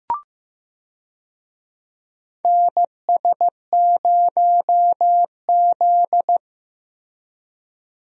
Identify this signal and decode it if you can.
Morse code, heard as a keyed tone, about 15 wpm, 700 Hz. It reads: NS0Z
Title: USA Call Signs - 4 Characters - Repeated 3x - 15wpm